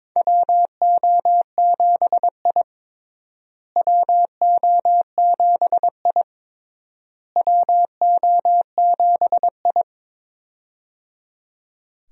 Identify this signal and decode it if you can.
Morse code, heard as a keyed tone, about 22 wpm, 700 Hz. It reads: WO7I WO7I WO7I
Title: USA Call Signs - 4 Characters - Repeated 3x - 22wpm